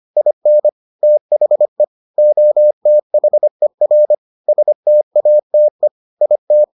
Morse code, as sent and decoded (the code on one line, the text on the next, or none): .. -. / - .... / --- - .... .-. / ... - .- - . / .. -
IN THE OTHER STATE IT